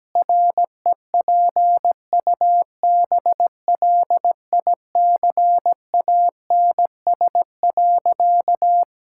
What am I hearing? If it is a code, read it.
Morse code, heard as a keyed tone, about 17 wpm, 700 Hz. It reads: REPUBLICANS.
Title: Pangram Challenge v2 - 17wpm